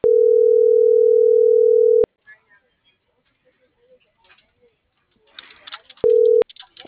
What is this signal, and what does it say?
Ringback tone (North America), 440 + 480 Hz.